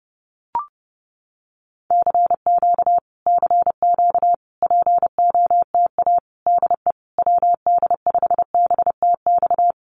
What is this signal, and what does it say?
Morse code, heard as a keyed tone, about 30 wpm, 700 Hz. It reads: CQ CQ POTA DE WD5BTX
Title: Common QSO Phrases Encore 30wpm